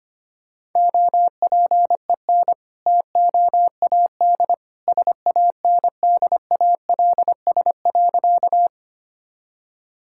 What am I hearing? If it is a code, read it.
Morse code, heard as a keyed tone, about 25 wpm, 700 Hz. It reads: OPEN TOAD SANDALS.
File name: practice / 25wpm / All-Age Appropriate Jokes - 25wpm